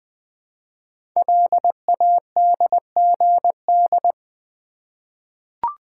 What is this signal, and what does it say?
Morse code, heard as a keyed tone, about 20 wpm, 700 Hz. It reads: LADGD
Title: Code Groups - 5 Letters - 20wpm